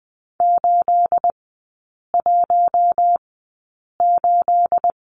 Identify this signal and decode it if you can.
Morse code, heard as a keyed tone, about 20 wpm, spaced as for 8 wpm, 690 Hz. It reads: 818
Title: 212 Lesson 34e - Number 8 - Three Characters 20wpm